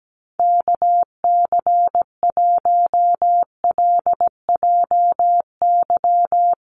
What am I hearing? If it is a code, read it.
Morse code, heard as a keyed tone, about 17 wpm, 700 Hz. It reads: KC1LJY
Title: USA Call Signs Repeated 3x 17wpm